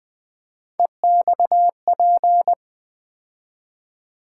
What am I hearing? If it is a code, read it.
Morse code, heard as a keyed tone, about 21 wpm, 700 Hz. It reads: EXP
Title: Build A Word - 3 Letters At A Time - 9-Letter Words - 20wpm